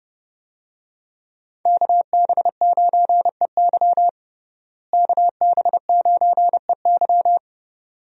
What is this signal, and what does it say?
Morse code, heard as a keyed tone, about 30 wpm, 700 Hz. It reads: KB9EY KB9EY
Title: USA Call Signs - 5 Characters - Repeated 2x - 30wpm